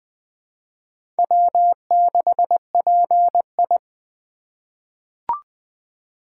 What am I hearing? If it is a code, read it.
Morse code, heard as a keyed tone, about 20 wpm, 700 Hz. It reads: W6PI
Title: USA Call Signs - 4 Characters - 3x Character Spacing - 20wpm